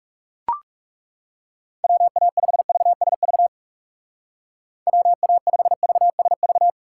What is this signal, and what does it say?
Morse code, heard as a keyed tone, about 36 wpm, 700 Hz. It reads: WA5VSV WA5VSV
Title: Speed Racing - USA Call Signs 30wpm